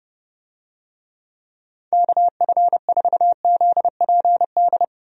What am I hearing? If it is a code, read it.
Morse code, heard as a keyed tone, about 31 wpm, 700 Hz. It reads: KF4ZPD
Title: USA Call Signs - 6 Characters - 30wpm